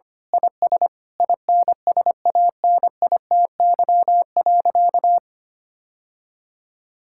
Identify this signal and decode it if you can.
Morse code, heard as a keyed tone, about 25 wpm, 700 Hz. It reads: IS INSANITY.
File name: practice / 25wpm / Sentences with 9-Word Phrases - 25wpm